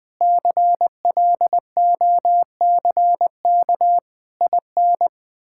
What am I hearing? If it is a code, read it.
Morse code, heard as a keyed tone, about 20 wpm, 700 Hz. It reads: CLOCK IN
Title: Pangram Challenge v2 - 20wpm